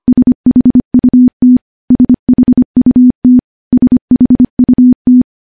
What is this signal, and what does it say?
Morse code, heard as a keyed tone, about 25 wpm, 250 Hz. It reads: SHUT SHUT SHUT
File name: morse / morse-training-25WPM